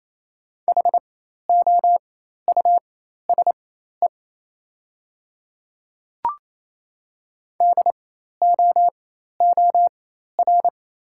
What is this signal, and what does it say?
Morse code, heard as a keyed tone, about 28 wpm, spaced as for 13 wpm, 700 Hz. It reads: HOUSE DOOR
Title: 4x Character Spacing - Top 500 Words - 28wpm